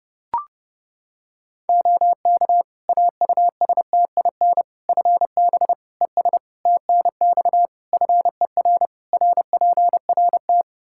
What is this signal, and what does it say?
Morse code, heard as a keyed tone, about 30 wpm, 700 Hz. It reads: OK AUSTIN FB ES TNX FER RPRT
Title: Practice QSO - Exchange 2 - Info - 30wpm